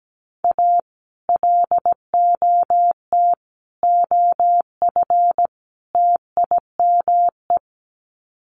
Morse code, heard as a keyed tone, about 17 wpm, 700 Hz. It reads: A LOT OF TIME
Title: Sets of 4 Words 17wpm